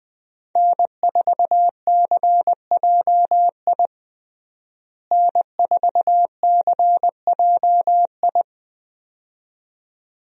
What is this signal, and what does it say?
Morse code, heard as a keyed tone, about 20 wpm, 700 Hz. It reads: N4CJI N4CJI
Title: USA Call Signs - 5 Characters - Repeated 2x - 20wpm